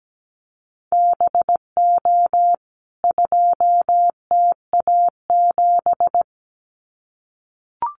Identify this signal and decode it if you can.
Morse code, heard as a keyed tone, about 17 wpm, 700 Hz. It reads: BO 2TA7
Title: CWT Exchange 17wpm